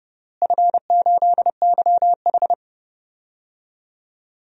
Morse code, heard as a keyed tone, about 32 wpm, 700 Hz. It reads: F8YH
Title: DXCC-100 Call Signs - 4 Characters - Repeated 3x - 30wpm